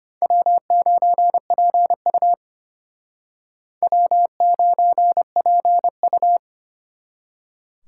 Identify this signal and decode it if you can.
Morse code, heard as a keyed tone, about 33 wpm, 700 Hz. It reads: W9PU W9PU